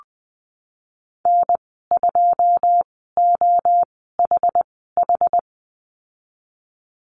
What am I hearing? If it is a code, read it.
Morse code, heard as a keyed tone, about 19 wpm, spaced as for 14 wpm, 700 Hz. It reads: N2OHH
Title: 088 Lesson 15g - Number 5 - Call Signs 20wpm